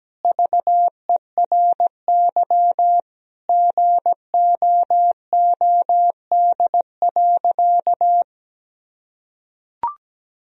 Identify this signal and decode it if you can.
Morse code, heard as a keyed tone, about 17 wpm, 700 Hz. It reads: VERY GOOD.
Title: Sentences from Top 1000 Words - Review Part 1 - 17wpm